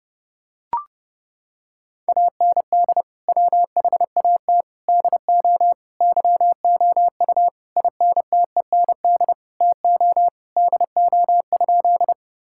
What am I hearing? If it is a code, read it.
Morse code, heard as a keyed tone, about 30 wpm, 700 Hz. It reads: AND WHAT DO YOU INTEND TO DO?
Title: Sentences from Top 1600 Words 30wpm